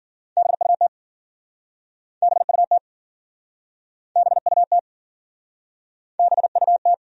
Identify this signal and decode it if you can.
Morse code, heard as a keyed tone, about 53 wpm, 700 Hz. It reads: BUT BUT BUT BUT